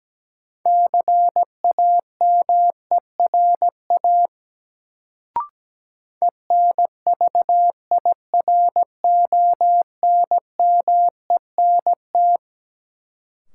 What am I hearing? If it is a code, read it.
Morse code, heard as a keyed tone, about 17 wpm, 700 Hz. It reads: CAMERA ENVIRONMENT
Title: Sentences from Top 1000 Words - Review Part 5 - New Word Drill - 17wpm